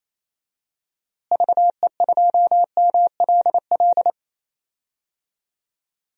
Morse code, heard as a keyed tone, about 28 wpm, 700 Hz. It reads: VE2MLL
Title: DXCC-100 Call Signs - 6 Characters - Repeated 2x - 28wpm